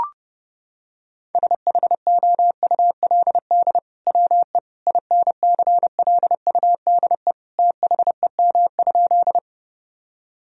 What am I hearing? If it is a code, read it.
Morse code, heard as a keyed tone, about 32 wpm, 700 Hz. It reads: SHOULD WE INCLUDE THEM?
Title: Sentences from Top 1000 Words - Review Part 3 - 30wpm